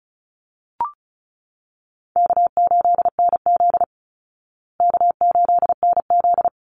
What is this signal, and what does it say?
Morse code, heard as a keyed tone, about 34 wpm, 700 Hz. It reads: K8NZ K8NZ
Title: Call Signs Repeated 3x - 35wpm